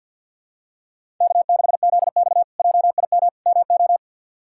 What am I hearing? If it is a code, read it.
Morse code, heard as a keyed tone, about 51 wpm, 690 Hz. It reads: K6ZX JIM MO
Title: K1USN SST QSO Practice - 50wpm